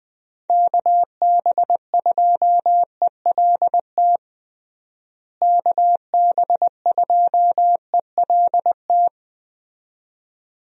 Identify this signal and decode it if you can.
Morse code, heard as a keyed tone, about 20 wpm, 700 Hz. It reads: KB2ELT KB2ELT
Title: USA Call Signs - 6 Characters - Repeated 2x - 20wpm